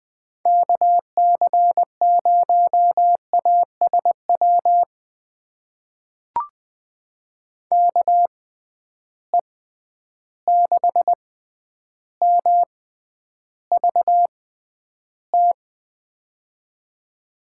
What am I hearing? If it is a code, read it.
Morse code, heard as a keyed tone, about 20 wpm, spaced as for 10 wpm, 700 Hz. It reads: KC0ASW KE6MVT
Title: USA Call Signs - 6 Characters - 6x Character Spacing - 20wpm